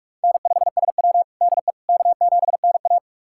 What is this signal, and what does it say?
Morse code, heard as a keyed tone, about 46 wpm, 700 Hz. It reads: N5SW DE K7NA
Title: USA Callsign DE Callsign - 4 Characters - 45wpm